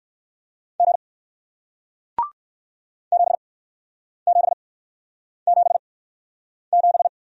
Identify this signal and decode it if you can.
Morse code, heard as a keyed tone, about 61 wpm, 700 Hz. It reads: G 7 7 7 7